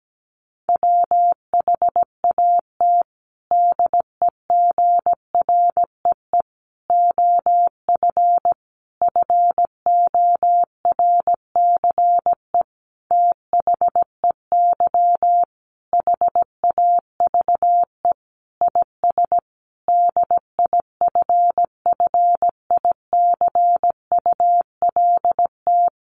Morse code, heard as a keyed tone, about 17 wpm, 700 Hz. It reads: WHAT DEGREE OF FORCE THEY HAVE IS DIFFICULT
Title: Sentences from Top 1500 Words 17wpm